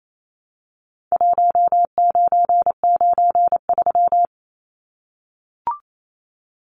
Morse code, heard as a keyed tone, about 28 wpm, 700 Hz. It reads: 1993
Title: Numbers - Contest Style - 4-Digit 28wpm